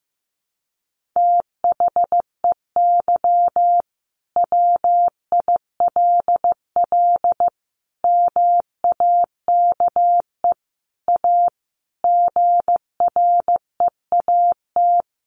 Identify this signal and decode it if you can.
Morse code, heard as a keyed tone, about 15 wpm, 700 Hz. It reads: THEY WILL MAKE A GREAT
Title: Sentences from Top 1500 Words 15wpm